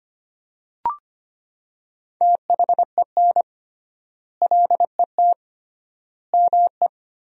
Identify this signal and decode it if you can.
Morse code, heard as a keyed tone, about 25 wpm, 700 Hz. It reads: THEN LET ME
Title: 3x Word-Spacing - Sentences from Top 300 Words - 25wpm